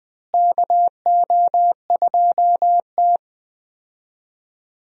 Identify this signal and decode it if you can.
Morse code, heard as a keyed tone, about 20 wpm, 700 Hz. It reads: KO2T
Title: USA Call Signs - 4 Characters - Repeated 3x - 20wpm